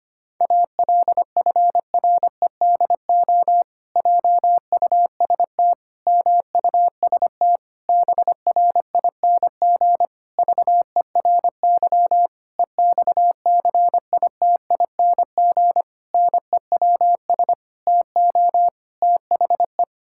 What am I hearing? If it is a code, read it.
Morse code, heard as a keyed tone, about 25 wpm, 700 Hz. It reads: ALFREDO JUST MUST BRING VERY EXCITING NEWS TO THE
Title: Pangram Challenge v2 - 25wpm